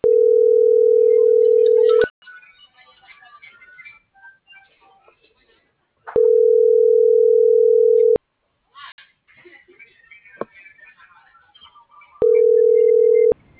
Ringback tone (North America), 440 + 480 Hz.